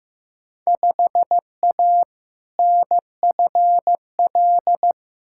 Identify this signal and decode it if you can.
Morse code, heard as a keyed tone, about 15 wpm, 700 Hz. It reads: 5A NFL
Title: Field Day Exchanges - 15wpm